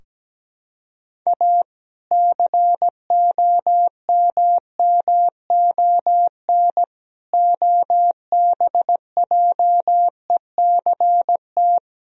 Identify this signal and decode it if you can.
Morse code, heard as a keyed tone, about 17 wpm, 700 Hz. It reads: A COMMON OBJECT
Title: Sentences from Top 1000 Words - Review Part 3 - 17wpm